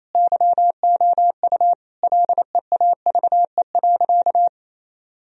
Morse code, heard as a keyed tone, about 28 wpm, 700 Hz. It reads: YOU LEAVE.
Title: Sentences from Top 1000 Words - Review Part 4 - 28wpm